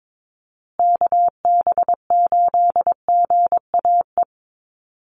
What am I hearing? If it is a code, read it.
Morse code, heard as a keyed tone, about 22 wpm, 700 Hz. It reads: KB8GAE
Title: Call Signs Repeated 3x - 22wpm